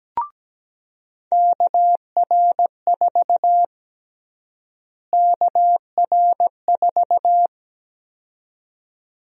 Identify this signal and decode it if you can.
Morse code, heard as a keyed tone, about 17 wpm, 700 Hz. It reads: KR4 KR4
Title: USA Call Sign Prefixes - Repeated 2x - 17wpm